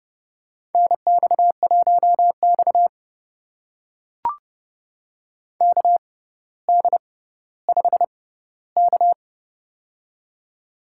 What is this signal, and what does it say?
Morse code, heard as a keyed tone, about 30 wpm, spaced as for 15 wpm, 700 Hz. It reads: NX1X KD5K
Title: USA Call Signs - 4 Characters - 6x Character Spacing - 30wpm